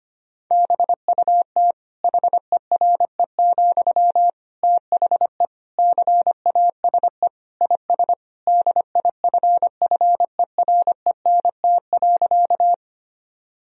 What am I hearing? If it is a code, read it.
Morse code, heard as a keyed tone, about 25 wpm, 700 Hz. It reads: BUT HERE, THE CASE IS DIFFERENT.
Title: Sentences from Top 1000 Words - Review Part 3 - 25wpm